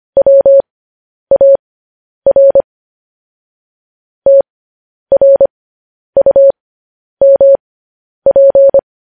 Morse code, heard as a keyed tone, about 25 wpm, spaced as for 10 wpm, 550 Hz. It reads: WAR TRUMP
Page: News Headlines in Morse Code at 10 WPM